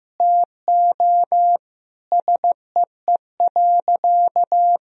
Morse code, heard as a keyed tone, about 15 wpm, 700 Hz. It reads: TO SEE.
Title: Sentences from Top 1000 Words - Review Part 3 - 15wpm